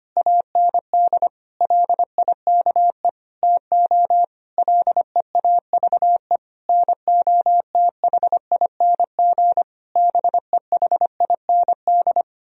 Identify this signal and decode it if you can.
Morse code, heard as a keyed tone, about 25 wpm, 700 Hz. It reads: AND LIKE TO LEAVE NOTHING BEHIND
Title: Sentences from Top 1000 Words - Review Part 1 - 25wpm